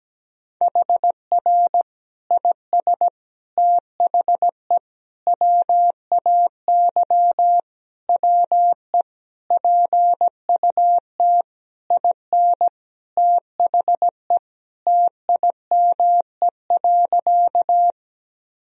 Morse code, heard as a keyed tone, about 17 wpm, 700 Hz. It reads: HR IS THE WAY WE PUT IN THE TIME.
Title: Sentences from Top 200 Words and Abbreviations 17wpm